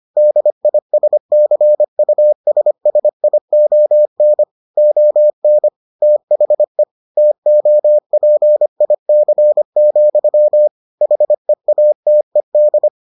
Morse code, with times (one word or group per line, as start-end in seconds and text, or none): 0.2-4.4 DISCUSSION
4.8-5.7 ON
6.0-6.8 THE
7.2-10.7 TOPIC,
11.0-12.9 HEATED